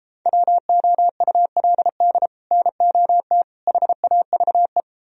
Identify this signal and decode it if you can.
Morse code, heard as a keyed tone, about 33 wpm, 700 Hz. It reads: WOULD NOT HAVE